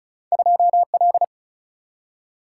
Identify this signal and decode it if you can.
Morse code, heard as a keyed tone, about 33 wpm, 700 Hz. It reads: 2L
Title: Single-Letter Single-Number 35wpm